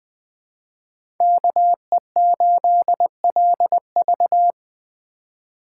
Morse code, heard as a keyed tone, about 20 wpm, 700 Hz. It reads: KE8LV
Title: USA Call Signs - 5 Characters - 2x Character Spacing - 20wpm